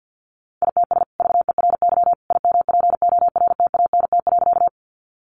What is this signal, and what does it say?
Morse code, heard as a keyed tone, about 50 wpm, 700 Hz. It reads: ITS VERY IMPORTANT.
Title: Sentences from Top 600 Words 50wpm